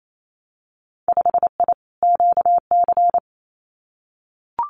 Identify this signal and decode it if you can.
Morse code, heard as a keyed tone, about 28 wpm, 700 Hz. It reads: 5I QC
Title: Winter Field Day Exchanges Encore 28wpm